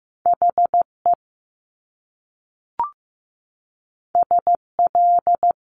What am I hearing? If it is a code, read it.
Morse code, heard as a keyed tone, about 15 wpm, 700 Hz. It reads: HE SL
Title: Top Two Letter Combinations in Words 15wpm